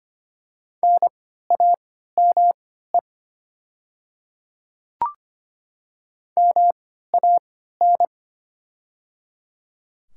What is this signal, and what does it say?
Morse code, heard as a keyed tone, about 25 wpm, spaced as for 14 wpm, 690 Hz. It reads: NAME MAN
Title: 3x Character Spacing - Top 500 Words - 25wpm